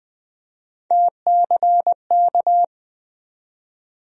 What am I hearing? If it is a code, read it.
Morse code, heard as a keyed tone, about 20 wpm, 700 Hz. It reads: TCK
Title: USA Call Sign Postfixes - 3 Letters 20wpm